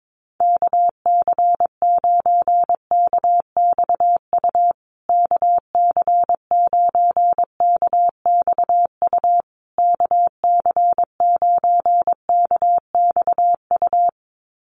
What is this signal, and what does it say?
Morse code, heard as a keyed tone, about 22 wpm, 700 Hz. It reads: KC9KXU KC9KXU KC9KXU
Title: USA Call Signs Repeated 3x 22wpm